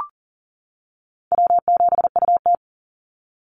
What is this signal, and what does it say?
Morse code, heard as a keyed tone, about 38 wpm, 700 Hz. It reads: W7UT
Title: Easy Call Signs - 40wpm